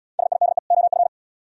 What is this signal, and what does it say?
Morse code, heard as a keyed tone, about 70 wpm, 700 Hz. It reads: SIDE OF